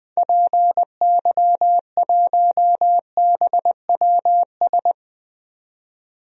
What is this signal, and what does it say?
Morse code, heard as a keyed tone, about 21 wpm, 700 Hz. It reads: PY1BWS
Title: DXCC-100 Call Signs - 6 Characters - 20wpm